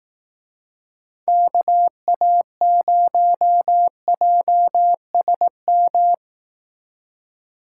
Morse code, heard as a keyed tone, about 18 wpm, 700 Hz. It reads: KA0JSM
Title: Speed Racing - USA 2x3 Call Signs 15wpm